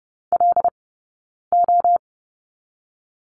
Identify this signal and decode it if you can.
Morse code, heard as a keyed tone, about 30 wpm, 700 Hz. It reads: L O
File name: Lesson 18d - Letter m - Two Characters 30wpm